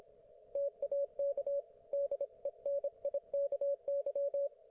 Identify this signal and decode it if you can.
Morse code, heard as a keyed tone, about 25 wpm, 560 Hz. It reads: TAK DENIKY
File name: Archiv / CUC194 / ok1fhi cul194